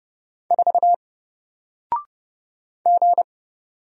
Morse code, heard as a keyed tone, about 31 wpm, 700 Hz. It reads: 4 G
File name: Single-Letter-Number-30wpm